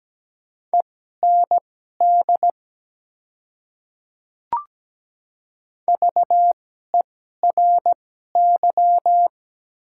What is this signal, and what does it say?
Morse code, heard as a keyed tone, about 17 wpm, spaced as for 12 wpm, 700 Hz. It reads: END VERY
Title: 2x Character Spacing - Top 300 Words - 17wpm